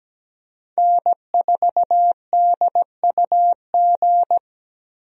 Morse code, heard as a keyed tone, about 17 wpm, 700 Hz. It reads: N4DUG
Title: USA Call Signs - 5 Characters - 17wpm